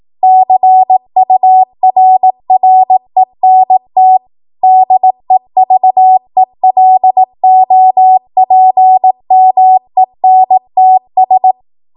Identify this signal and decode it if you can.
Morse code, heard as a keyed tone, about 18 wpm, 750 Hz. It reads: CURRENT DEVELOPMENTS
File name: Morse / 18WPM